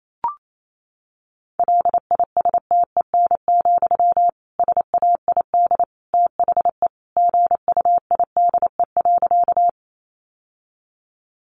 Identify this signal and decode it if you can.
Morse code, heard as a keyed tone, about 28 wpm, 700 Hz. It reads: LISTEN, SAID THE GUIDE.
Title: Sentences from Top 600 Words 28wpm